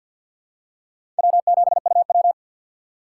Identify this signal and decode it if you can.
Morse code, heard as a keyed tone, about 52 wpm, 690 Hz. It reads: W7UW
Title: USA Call Signs - 2x Character Spacing - 50wpm